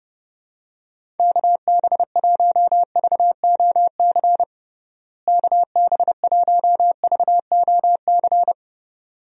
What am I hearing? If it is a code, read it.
Morse code, heard as a keyed tone, about 30 wpm, 700 Hz. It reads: KB1VOC KB1VOC
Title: USA Call Signs - 6 Characters - Repeated 2x - 30wpm